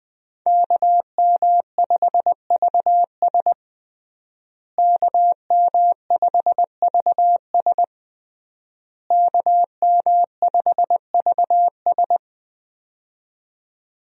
Morse code, heard as a keyed tone, about 20 wpm, 700 Hz. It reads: KM5VS KM5VS KM5VS
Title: USA Call Signs - 5 Characters - Repeated 3x - 20wpm